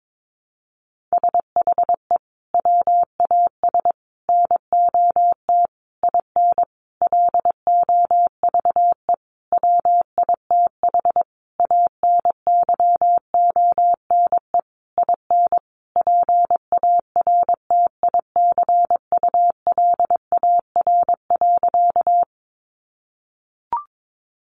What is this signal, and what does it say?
Morse code, heard as a keyed tone, about 22 wpm, 700 Hz. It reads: SHE WAS NOT IN LOVE WITH ANYONE IN PARTICULAR.
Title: Sentences from Top 900 Words 22wpm